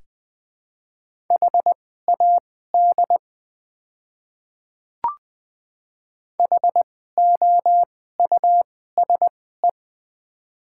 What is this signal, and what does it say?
Morse code, heard as a keyed tone, about 20 wpm, spaced as for 15 wpm, 700 Hz. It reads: HAD HOUSE
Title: 2x Character Spacing - Top 200 Words - 20wpm